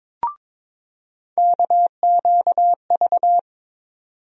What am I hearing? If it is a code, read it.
Morse code, heard as a keyed tone, about 22 wpm, 700 Hz. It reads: KQV
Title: USA Call Sign Postfixes - 3 Letters 22wpm